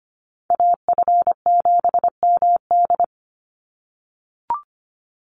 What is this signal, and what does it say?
Morse code, heard as a keyed tone, about 25 wpm, 700 Hz. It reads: AF7MD
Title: NA SOTA Chaser Callsigns 25wpm